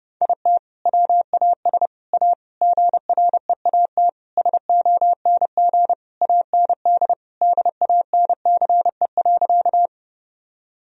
Morse code, heard as a keyed tone, about 30 wpm, 700 Hz. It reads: IT WAS A GREAT SONG AND DANCE.
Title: Sentences from Top 1000 Words - Review Part 4 - 30wpm